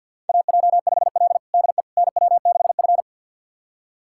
Morse code, heard as a keyed tone, about 51 wpm, 700 Hz. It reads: AJ5L DE NW6F
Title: USA Callsign DE Callsign - 4 Characters - 50wpm